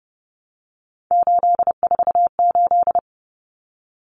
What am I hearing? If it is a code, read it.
Morse code, heard as a keyed tone, about 28 wpm, 700 Hz. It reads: 848
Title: Common Cut Numbers - 3-Digit 30wpm